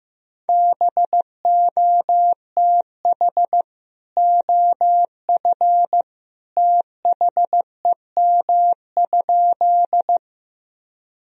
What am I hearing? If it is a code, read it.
Morse code, heard as a keyed tone, about 16 wpm, 700 Hz. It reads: BOTH OF THEM?
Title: Sentences from Top 1000 Words - Review Part 1 - 15wpm